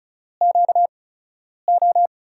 Morse code, heard as a keyed tone, about 36 wpm, spaced as for 10 wpm, 700 Hz. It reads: QO
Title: USA Call Signs - 4 Characters - 8x Character Spacing - 35wpm